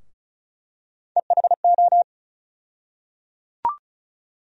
Morse code, heard as a keyed tone, about 36 wpm, 700 Hz. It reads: EHO
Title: USA Call Sign Postfixes - 3 Letters 35wpm